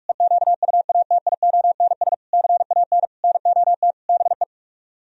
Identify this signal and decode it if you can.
Morse code, heard as a keyed tone, about 49 wpm, 700 Hz. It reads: EQUATIONS CAN NOT BE